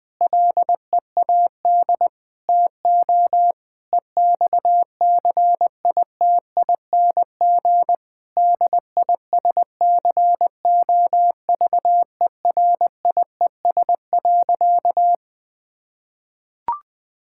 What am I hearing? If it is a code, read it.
Morse code, heard as a keyed tone, about 20 wpm, 700 Hz. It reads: LEAD TO EXCITING DISCOVERIES.
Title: Difficult Sentences with Hints - 20wpm